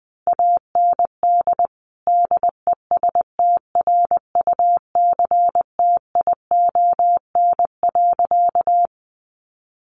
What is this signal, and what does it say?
Morse code, heard as a keyed tone, about 20 wpm, 700 Hz. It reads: AND DESTRUCTION.
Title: A Sound Of Thunder 20wpm